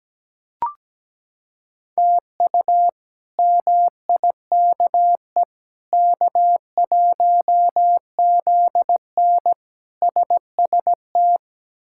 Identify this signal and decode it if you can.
Morse code, heard as a keyed tone, about 17 wpm, 700 Hz. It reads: TU MIKE K1ZN SST
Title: K1USN SST QSO Practice - 17wpm